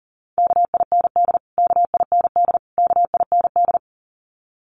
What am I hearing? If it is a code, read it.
Morse code, heard as a keyed tone, about 40 wpm, 700 Hz. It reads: KIND KIND KIND
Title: Top 500 Words Repeated 40wpm